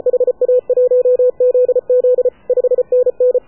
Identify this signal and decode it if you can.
Morse code, heard as a keyed tone, about 33 wpm, 500 Hz. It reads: HA1ZZ 5NN
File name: AUDIO2019 / HA7UI CW